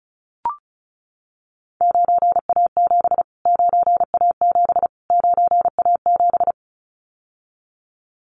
Morse code, heard as a keyed tone, about 34 wpm, 700 Hz. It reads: 9A7 9A7 9A7
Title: Callsign Prefixes for Top 25 DXCC - Repeated 3x - 35wpm